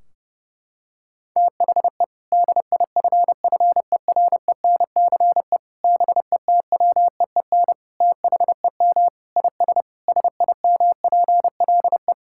Morse code, heard as a keyed tone, about 30 wpm, 700 Hz. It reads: THE DIFFERENCE BETWEEN THEM IS SIMPLE
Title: Sentences from Top 1600 Words 30wpm